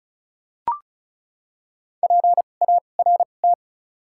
Morse code, heard as a keyed tone, about 34 wpm, spaced as for 25 wpm, 700 Hz. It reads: PART
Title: 2x Character Spacing - Top 200 Words - 35wpm